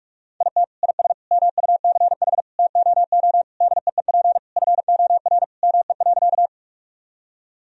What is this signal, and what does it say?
Morse code, heard as a keyed tone, about 41 wpm, 700 Hz. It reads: IT IS MUCH TOO DEEP FOR ME.
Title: Sentences from Top 1000 Words - Review Part 3 - 45wpm